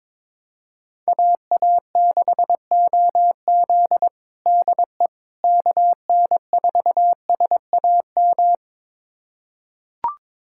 Morse code, heard as a keyed tone, about 22 wpm, 700 Hz. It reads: AA6OZ DE KN4SAM
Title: USA Callsign DE Callsign 22wpm